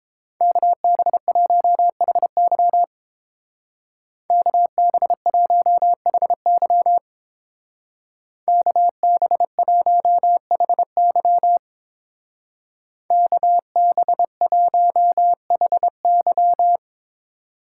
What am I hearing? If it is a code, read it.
Morse code, heard as a keyed tone, about 27 wpm, 700 Hz. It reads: KB1HY KB1HY KB1HY KB1HY
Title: Speed Racing - USA 2x2 Call Signs 22wpm